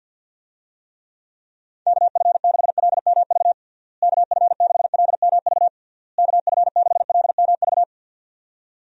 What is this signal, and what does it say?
Morse code, heard as a keyed tone, about 52 wpm, 700 Hz. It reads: KF6LMV KF6LMV KF6LMV
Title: USA Call Signs - 6 Characters - Repeated 3x - 50wpm